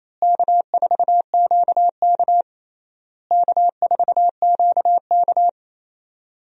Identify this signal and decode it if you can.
Morse code, heard as a keyed tone, about 28 wpm, 700 Hz. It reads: K4QK K4QK
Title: USA Call Signs - 4 Characters - Repeated 3x - 28wpm